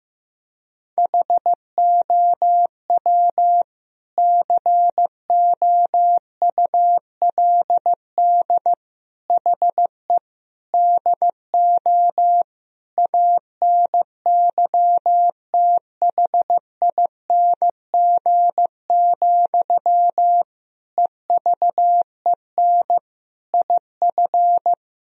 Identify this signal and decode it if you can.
Morse code, heard as a keyed tone, about 15 wpm, 700 Hz. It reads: HOW COULD HE DO ANYTHING, EVEN IF
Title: Sentences from Top 1000 Words - Review Part 1 - 15wpm